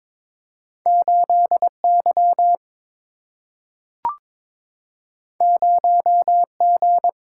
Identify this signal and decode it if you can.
Morse code, heard as a keyed tone, about 22 wpm, 700 Hz. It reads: 8 Y 0 G
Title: Single-Letter Single-Number 22wpm